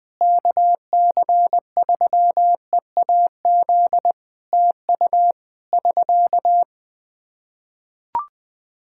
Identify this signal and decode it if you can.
Morse code, heard as a keyed tone, about 20 wpm, 700 Hz. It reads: KC3EAZ TU <SK>
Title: Practice QSO - Exchange 4 - Ending - 20wpm